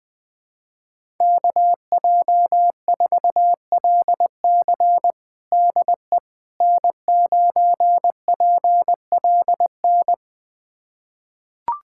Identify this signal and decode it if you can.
Morse code, heard as a keyed tone, about 20 wpm, 700 Hz. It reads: KJ4LC DE N9PLN
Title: USA Callsign DE Callsign - 5 Characters - 20wpm